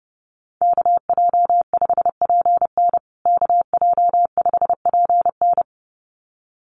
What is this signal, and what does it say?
Morse code, heard as a keyed tone, about 30 wpm, 700 Hz. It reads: KJ5PN KJ5PN
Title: USA Call Signs Repeated 2x 30wpm